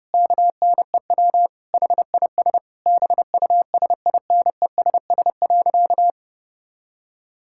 Morse code, heard as a keyed tone, about 32 wpm, 700 Hz. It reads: KNEW HIS BUSINESS.
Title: Sentences from Top 1600 Words 30wpm